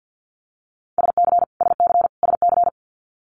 Morse code, heard as a keyed tone, about 50 wpm, 700 Hz. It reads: SC SC SC
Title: Multi-letter Phonemes Repeated 3x 50wpm